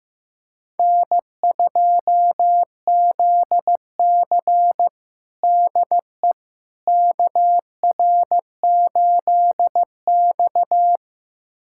Morse code, heard as a keyed tone, about 15 wpm, 700 Hz. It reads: N2ZC DE KR8X
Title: USA Callsign DE Callsign - 4 Characters - 15wpm